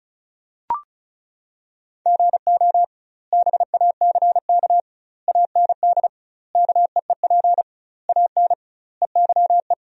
Morse code, heard as a keyed tone, about 34 wpm, 700 Hz. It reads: GO BACK AND KEEP AN EYE
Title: 2x Word-Spacing - Sentences from Top 300 Words - 35wpm